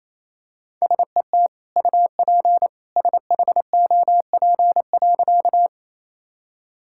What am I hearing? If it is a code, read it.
Morse code, heard as a keyed tone, about 28 wpm, 700 Hz. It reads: SET UP SHOP.
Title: Sentences from Top 1600 Words 28wpm